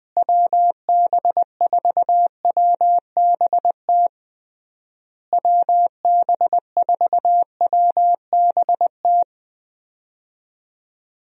Morse code, heard as a keyed tone, about 20 wpm, 700 Hz. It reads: WB4WBT WB4WBT
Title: USA Call Signs - 6 Characters - Repeated 3x - 20wpm